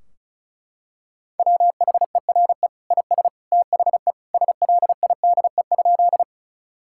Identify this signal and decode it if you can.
Morse code, heard as a keyed tone, about 36 wpm, 700 Hz. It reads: WHERE IS THE SLIDE?
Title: Sentences from Top 1600 Words 35wpm